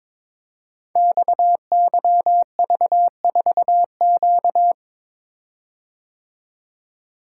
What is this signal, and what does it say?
Morse code, heard as a keyed tone, about 23 wpm, 700 Hz. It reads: XYV4Q
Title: Code Groups - 5 Letter:Numbers - 22wpm